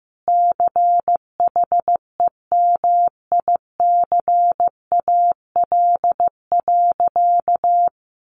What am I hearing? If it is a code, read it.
Morse code, heard as a keyed tone, about 15 wpm, 700 Hz. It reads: CHEMICAL.
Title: Sentences from Top 1200 Words 15wpm